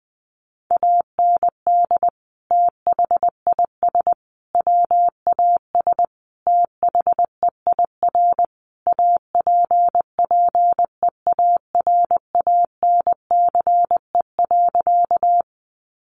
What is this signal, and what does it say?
Morse code, heard as a keyed tone, about 20 wpm, 700 Hz. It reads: AND THIS WAS THEIR APPEARANCE.
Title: Sentences from Top 1100 Words 20wpm